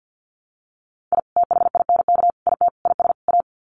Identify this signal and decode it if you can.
Morse code, heard as a keyed tone, about 50 wpm, 700 Hz. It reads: I THINK IT IS A
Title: Sets of 5 Words Encore 50wpm